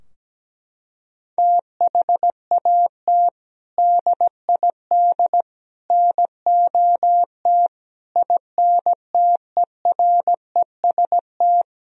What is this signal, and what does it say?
Morse code, heard as a keyed tone, about 17 wpm, 700 Hz. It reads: THAT DID NOT INTEREST
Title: Sentences from Top 1000 Words - Review Part 2 - 17wpm